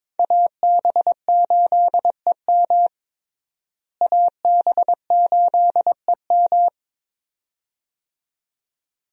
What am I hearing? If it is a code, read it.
Morse code, heard as a keyed tone, about 22 wpm, 700 Hz. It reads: AB8EM AB8EM
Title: USA Call Signs - 5 Characters - Repeated 2x - 22wpm